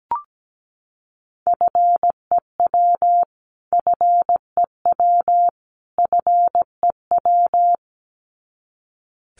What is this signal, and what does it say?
Morse code, heard as a keyed tone, about 17 wpm, 700 Hz. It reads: FEW FEW FEW
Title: Top 500 Words Repeated 17wpm